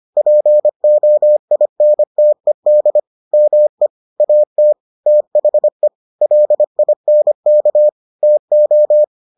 Morse code, heard as a keyed tone, about 25 wpm, 600 Hz. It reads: POINTED ME AT THE LINK TO